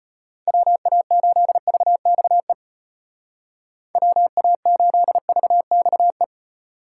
Morse code, heard as a keyed tone, about 34 wpm, 700 Hz. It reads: WA8VXE WA8VXE